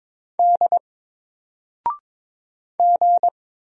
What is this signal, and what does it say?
Morse code, heard as a keyed tone, about 22 wpm, 700 Hz. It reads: D G
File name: Single-Letter-Number-22wpm